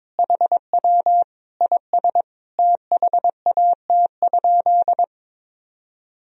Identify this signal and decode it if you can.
Morse code, heard as a keyed tone, about 22 wpm, 700 Hz. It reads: HW IS THAT?
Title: Sentences from Top 100 Words and Abbreviations 22wpm